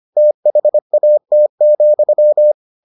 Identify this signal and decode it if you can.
Morse code, heard as a keyed tone, about 25 wpm, 600 Hz. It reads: THAT,